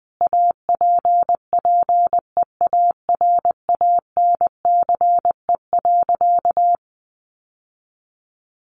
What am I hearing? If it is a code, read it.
Morse code, heard as a keyed tone, about 20 wpm, 700 Hz. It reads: APPEARANCE.
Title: Sentences from Top 1100 Words 20wpm